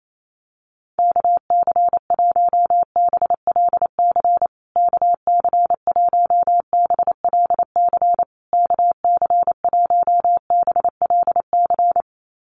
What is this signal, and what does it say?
Morse code, heard as a keyed tone, about 28 wpm, 700 Hz. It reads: KC1BLC KC1BLC KC1BLC
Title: USA Call Signs Repeated 3x 28wpm